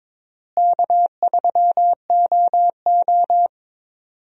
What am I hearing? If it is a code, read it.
Morse code, heard as a keyed tone, about 22 wpm, 700 Hz. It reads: K3OO
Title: USA Call Signs - 4 Characters - Repeated 3x - 22wpm